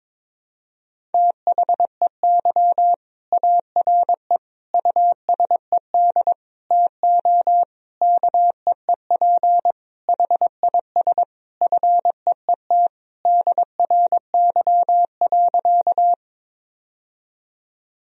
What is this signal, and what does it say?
Morse code, heard as a keyed tone, about 21 wpm, 700 Hz. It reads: THEY ARE USED TO KEEP HIS FEET DRY.
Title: Sentences from Top 1000 Words - Review Part 3 - 22wpm